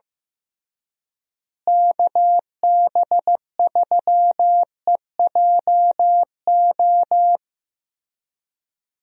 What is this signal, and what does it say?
Morse code, heard as a keyed tone, about 16 wpm, 700 Hz. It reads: KB3EJO
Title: USA Call Signs - 6 Characters - Repeated 3x - 15wpm